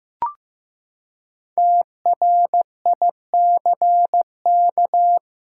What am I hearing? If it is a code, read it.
Morse code, heard as a keyed tone, about 15 wpm, 700 Hz. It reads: TRICK
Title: Five Letter Words - 15wpm